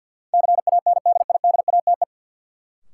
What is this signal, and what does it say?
Morse code, heard as a keyed tone, about 48 wpm, 700 Hz. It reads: CANDIDATE
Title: Sentences from Top 1600 Words - New Word Drill - 50wpm